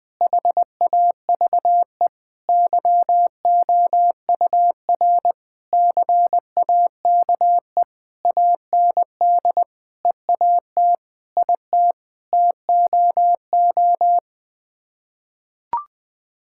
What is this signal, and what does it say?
Morse code, heard as a keyed tone, about 21 wpm, 700 Hz. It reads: HAVE YOUR CAKE AND EAT IT TOO
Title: Sets of 9 Words - 20wpm